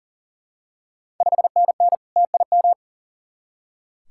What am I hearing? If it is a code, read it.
Morse code, heard as a keyed tone, about 39 wpm, 690 Hz. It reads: 5NN TIM
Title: Ontario QSO Party Exchange 40wpm